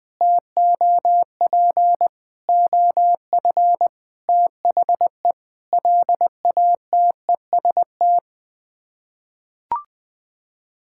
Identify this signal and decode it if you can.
Morse code, heard as a keyed tone, about 21 wpm, 700 Hz. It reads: TOP OF THE LATEST
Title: Sets of 6 Words - 20wpm